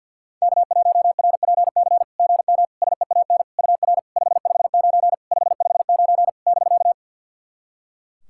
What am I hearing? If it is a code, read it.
Morse code, heard as a keyed tone, about 52 wpm, 690 Hz. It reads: K1RPC GM SEAN UR 559 559 <BK>
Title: SOTA QSO Practice 50wpm